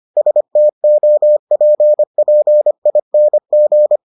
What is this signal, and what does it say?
Morse code, heard as a keyed tone, about 25 wpm, 600 Hz. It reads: STOPPING